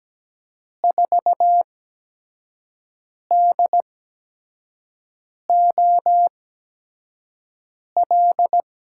Morse code, heard as a keyed tone, about 17 wpm, spaced as for 5 wpm, 700 Hz. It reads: 4DOL
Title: USA Call Signs - 8x Character Spacing - 17wpm